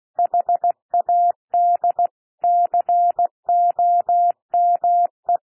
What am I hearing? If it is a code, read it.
Morse code, heard as a keyed tone, about 16 wpm, 700 Hz. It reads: HAD COME
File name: cwmp3 / P1 SOUND PASSAGE 16 wpm